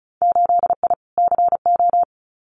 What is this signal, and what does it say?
Morse code, heard as a keyed tone, about 36 wpm, 700 Hz. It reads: 8I CO
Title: Winter Field Day Exchanges Encore 35wpm